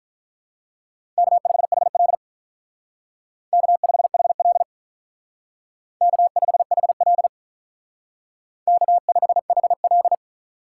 Morse code, heard as a keyed tone, about 42 wpm, 700 Hz. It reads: K5HL K5HL K5HL K5HL
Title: Speed Racing - USA 1x2 Call Signs 35wpm